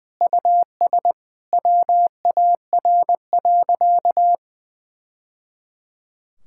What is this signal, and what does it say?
Morse code, heard as a keyed tone, about 21 wpm, 700 Hz. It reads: US WAR.
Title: Sentences from Top 1000 Words - Review Part 6 - 20wpm